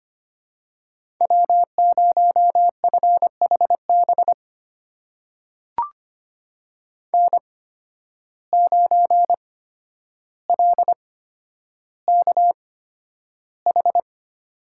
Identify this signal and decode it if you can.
Morse code, heard as a keyed tone, about 25 wpm, spaced as for 11 wpm, 700 Hz. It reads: W0FHB N9LKH
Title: USA Call Signs - 5 Characters - 8x Character Spacing - 25wpm